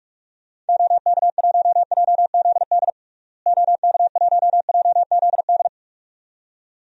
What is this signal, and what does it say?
Morse code, heard as a keyed tone, about 46 wpm, 700 Hz. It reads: OK1JZD OK1JZD
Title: DXCC-100 Call Signs - 6 Characters - Repeated 2x - 45wpm